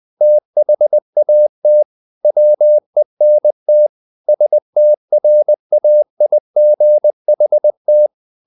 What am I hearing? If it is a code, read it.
Morse code, heard as a keyed tone, about 20 wpm, 600 Hz. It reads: THAT WENT STRAIGHT
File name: the old man and the sea / the old man and the sea part 1 20wpm